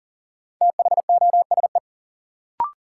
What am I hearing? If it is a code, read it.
Morse code, heard as a keyed tone, about 40 wpm, 700 Hz. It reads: THOSE
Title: Sentences from Top 300 Words - New Word Drill - 40wpm